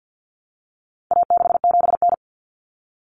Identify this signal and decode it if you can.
Morse code, heard as a keyed tone, about 50 wpm, 700 Hz. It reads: A67N
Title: Common Cut Numbers - 4-Digit 50wpm